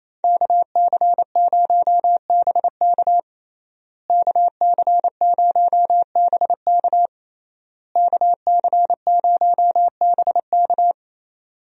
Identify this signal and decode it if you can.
Morse code, heard as a keyed tone, about 28 wpm, 700 Hz. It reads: KC0BK KC0BK KC0BK
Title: USA Call Signs - 5 Characters - Repeated 3x - 28wpm